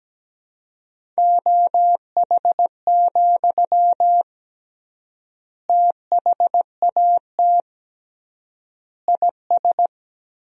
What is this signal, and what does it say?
Morse code, heard as a keyed tone, about 17 wpm, 700 Hz. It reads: OH, THAT IS
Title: 3x Word-Spacing - Sentences from Top 500 Words - 17wpm